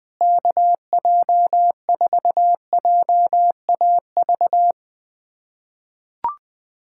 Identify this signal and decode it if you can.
Morse code, heard as a keyed tone, about 21 wpm, 700 Hz. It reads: KJ4JAV
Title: USA Callsign DE Callsign - 6 Characters - 20wpm